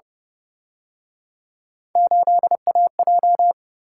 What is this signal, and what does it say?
Morse code, heard as a keyed tone, about 29 wpm, 700 Hz. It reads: 8AJ